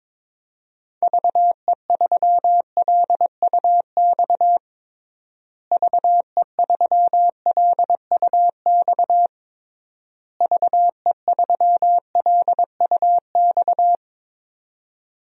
Morse code, heard as a keyed tone, about 22 wpm, 700 Hz. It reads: VE3LUX VE3LUX VE3LUX
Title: DXCC-100 Call Signs - 6 Characters - Repeated 3x - 22wpm